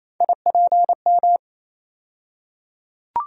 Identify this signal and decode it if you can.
Morse code, heard as a keyed tone, about 28 wpm, 700 Hz. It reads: IPM
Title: USA Call Sign Postfixes - 3 Letters 28wpm